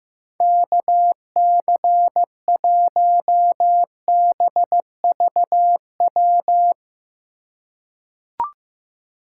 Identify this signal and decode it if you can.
Morse code, heard as a keyed tone, about 15 wpm, 700 Hz. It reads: KC1BVW
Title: USA Call Signs - 6 Characters - 3x Character Spacing - 15wpm